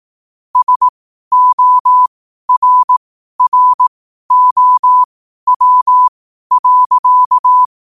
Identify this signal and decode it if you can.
Morse code, heard as a keyed tone, about 18 wpm, spaced as for 12 wpm, 1000 Hz. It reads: SORROW.